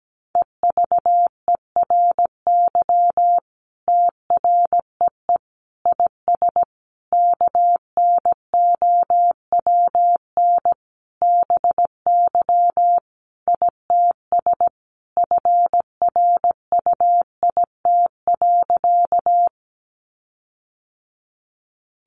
Morse code, heard as a keyed tone, about 17 wpm, 700 Hz. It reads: EVERY TREE IS KNOWN BY ITS FRUIT.
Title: Sentences from Top 1300 Words 17wpm